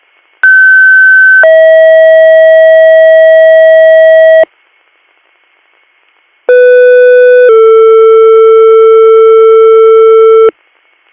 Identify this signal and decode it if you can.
Two-tone page, heard as a one-second tone, then a three-second tone, about 1550 then 640 Hz, about 500 then 450 Hz. It is heard from 0.4 to 10.5 s.